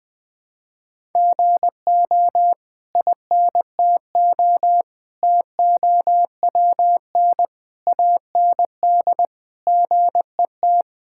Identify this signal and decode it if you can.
Morse code, heard as a keyed tone, about 20 wpm, 700 Hz. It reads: GO INTO TOWN AND GET
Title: Sentences from Top 1000 Words - Review Part 6 - 20wpm